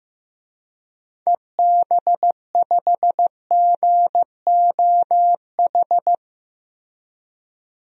Morse code, heard as a keyed tone, about 15 wpm, 700 Hz. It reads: EB5GOH
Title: DXCC-100 Call Signs - 6 Characters - Repeated 2x - 15wpm